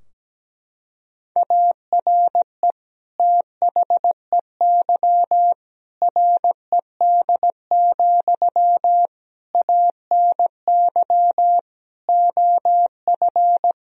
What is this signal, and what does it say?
Morse code, heard as a keyed tone, about 17 wpm, 700 Hz. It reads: ARE THEY RED, ANY OF
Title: Sentences from Top 1000 Words - Review Part 2 - 17wpm